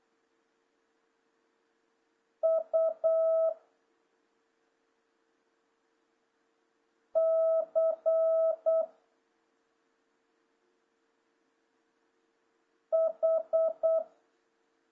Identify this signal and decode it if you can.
Morse code, heard as a keyed tone, about 8 wpm, 650 Hz. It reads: U C H
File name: jh8cbh / 2026cw